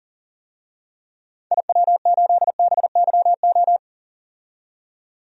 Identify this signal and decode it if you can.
Morse code, heard as a keyed tone, about 40 wpm, 700 Hz. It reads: IW8BYO
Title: DXCC-100 Call Signs - 6 Characters - 40wpm